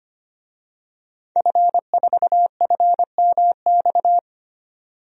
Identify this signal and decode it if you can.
Morse code, heard as a keyed tone, about 25 wpm, 700 Hz. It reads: F4FMX
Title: DXCC-100 Call Signs - 5 Characters - 25wpm